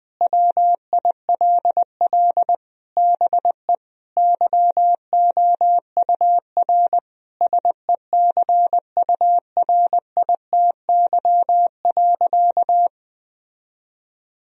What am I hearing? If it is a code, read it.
Morse code, heard as a keyed tone, about 20 wpm, 700 Hz. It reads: WILL BE YOUR SECURITY.
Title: Sentences from Top 1000 Words - Review Part 4 - 20wpm